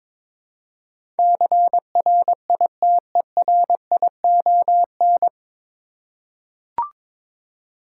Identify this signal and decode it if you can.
Morse code, heard as a keyed tone, about 21 wpm, 700 Hz. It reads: CRITERION
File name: 9 Letter Words - First 3 Letters Given - 22wpm